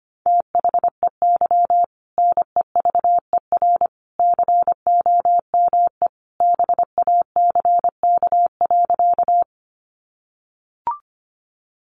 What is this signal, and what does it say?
Morse code, heard as a keyed tone, about 25 wpm, 700 Hz. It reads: THEY NEVER COME BACK.
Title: Sentences from Top 300 Words 25wpm